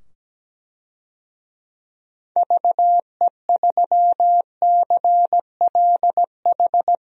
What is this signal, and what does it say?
Morse code, heard as a keyed tone, about 17 wpm, 700 Hz. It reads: VE3CLH
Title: DXCC-100 Call Signs - 6 Characters - 17wpm